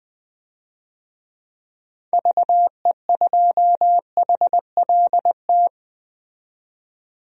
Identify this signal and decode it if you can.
Morse code, heard as a keyed tone, about 20 wpm, 700 Hz. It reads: VE2HLT
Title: DXCC-100 Call Signs - 6 Characters - 20wpm